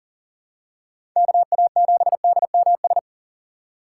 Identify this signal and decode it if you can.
Morse code, heard as a keyed tone, about 40 wpm, 700 Hz. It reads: KA7DMS
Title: USA Call Signs - 6 Characters - 6x Character Spacing - 40wpm